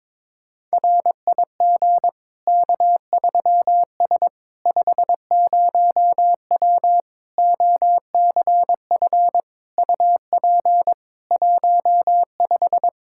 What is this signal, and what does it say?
Morse code, heard as a keyed tone, about 22 wpm, 700 Hz. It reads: RIG K3S 50W OCF UP 15
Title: K1USN SST QSO Practice 22wpm